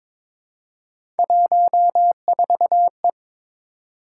Morse code, heard as a keyed tone, about 23 wpm, 700 Hz. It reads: 14E
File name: practice / 22wpm / Three Letter-Number Combinations - 22wpm